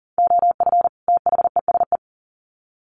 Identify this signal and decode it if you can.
Morse code, heard as a keyed tone, about 40 wpm, 700 Hz. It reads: OF THESE
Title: Sets of 2 Words 40wpm